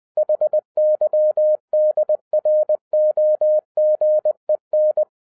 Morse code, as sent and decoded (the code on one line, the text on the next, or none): .... -.-- -.. .-. --- --. -.
HYDROGEN